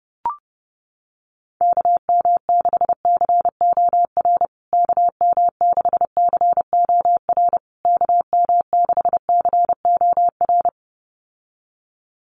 Morse code, heard as a keyed tone, about 28 wpm, 700 Hz. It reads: KM6COR KM6COR KM6COR
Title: USA Call Signs Repeated 3x 30wpm